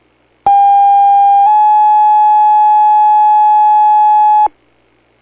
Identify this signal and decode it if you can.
Two-tone page, heard as a one-second tone, then a three-second tone, about 800 then 850 Hz. It is heard from 0.5 to 4.5 s.